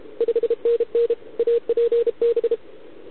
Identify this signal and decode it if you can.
Morse code, heard as a keyed tone, about 31 wpm, 470 Hz. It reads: 5NN APB